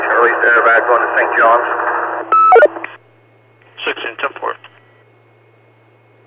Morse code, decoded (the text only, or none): none